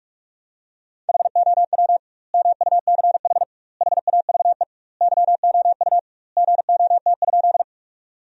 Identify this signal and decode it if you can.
Morse code, heard as a keyed tone, about 46 wpm, 700 Hz. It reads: HOW MUCH HAVE YOU GOT?
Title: 2x Word-Spacing - Sentences from Top 300 Words - 45wpm